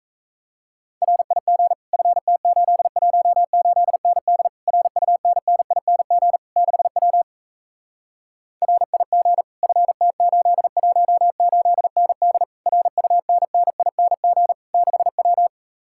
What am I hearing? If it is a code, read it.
Morse code, heard as a keyed tone, about 40 wpm, 700 Hz. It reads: RIG FT818ND RUNNING 6W RIG FT818ND RUNNING 6W